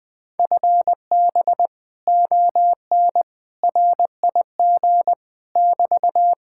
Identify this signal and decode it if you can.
Morse code, heard as a keyed tone, about 20 wpm, 700 Hz. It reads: FB ON RIG <BT>